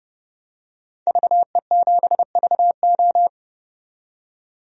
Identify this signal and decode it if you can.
Morse code, heard as a keyed tone, about 30 wpm, 700 Hz. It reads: VE7VO
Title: DXCC-100 Call Signs - 5 Characters - 30wpm